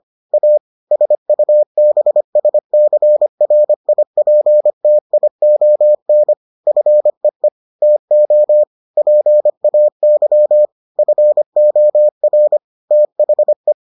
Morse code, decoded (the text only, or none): A SUBSCRIPTION FEE TO PAY FOR THE